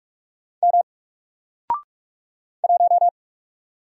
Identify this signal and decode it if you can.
Morse code, heard as a keyed tone, about 44 wpm, 700 Hz. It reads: M 1
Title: Single Letter-Number Alt 45wpm